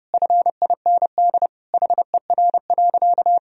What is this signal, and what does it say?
Morse code, heard as a keyed tone, about 30 wpm, 700 Hz. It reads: FIND HER.
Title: Sentences from Top 1000 Words - Review Part 2 - 30wpm